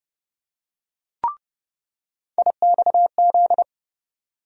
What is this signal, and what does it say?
Morse code, heard as a keyed tone, about 30 wpm, 700 Hz. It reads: IXZ
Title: USA Call Sign Postfixes - 3 Letters 30wpm